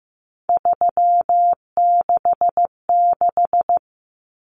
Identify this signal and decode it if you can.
Morse code, heard as a keyed tone, about 15 wpm, 700 Hz. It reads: 366
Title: Numbers - 3-Digit 15wpm